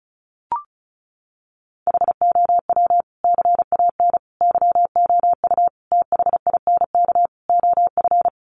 Morse code, heard as a keyed tone, about 34 wpm, 700 Hz. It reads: HOW CAN YOU THINK OF
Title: Sentences from Top 200 Words 35wpm